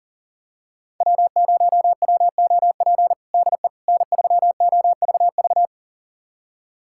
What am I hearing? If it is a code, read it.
Morse code, heard as a keyed tone, about 37 wpm, 700 Hz. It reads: W0WOP DE N3OVV
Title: USA Callsign DE Callsign - 5 Characters - 40wpm